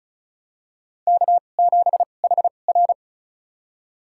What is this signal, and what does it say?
Morse code, heard as a keyed tone, about 36 wpm, spaced as for 25 wpm, 700 Hz. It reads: K7HR
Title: USA Call Signs - 4 Characters - 2x Character Spacing - 35wpm